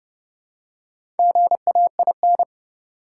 Morse code, heard as a keyed tone, about 30 wpm, 700 Hz. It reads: GAIN
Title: Four Letter Words - 30wpm